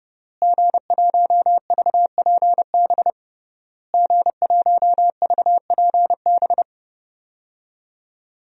Morse code, heard as a keyed tone, about 30 wpm, 700 Hz. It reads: G1VPB G1VPB
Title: DXCC-100 Call Signs - 5 Characters - Repeated 3x - 30wpm